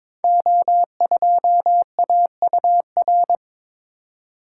Morse code, heard as a keyed tone, about 22 wpm, 700 Hz. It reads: O2AUR
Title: Code Groups - 5 Letter:Numbers - 22wpm